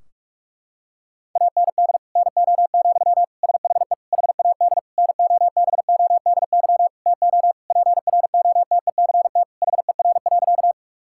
Morse code, heard as a keyed tone, about 46 wpm, 700 Hz. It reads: AND NO, SHE HAD NOBODY TO PROTECT HER.
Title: Sentences from Top 1600 Words 45wpm